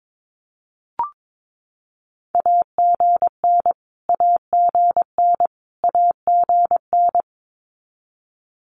Morse code, heard as a keyed tone, about 22 wpm, 700 Hz. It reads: AGN AGN AGN
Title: Abbreviations Repeated 3x - 22wpm